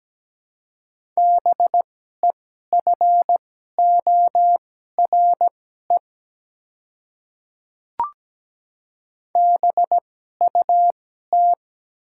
Morse code, heard as a keyed tone, about 17 wpm, spaced as for 12 wpm, 700 Hz. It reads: BEFORE BUT
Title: 2x Character Spacing - Top 200 Words - 17wpm